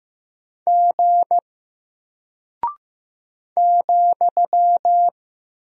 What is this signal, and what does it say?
Morse code, heard as a keyed tone, about 15 wpm, 700 Hz. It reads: G ,
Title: Single Character (Letter:Number:Punctuation:Prosign) Alt 15wpm